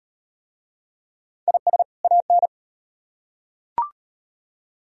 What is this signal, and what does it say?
Morse code, heard as a keyed tone, about 37 wpm, 700 Hz. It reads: IS AN